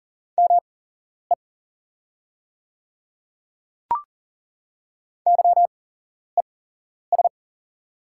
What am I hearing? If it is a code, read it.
Morse code, heard as a keyed tone, about 39 wpm, spaced as for 11 wpm, 690 Hz. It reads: ME YES
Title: 8x Character Spacing - Top 500 Words - 40wpm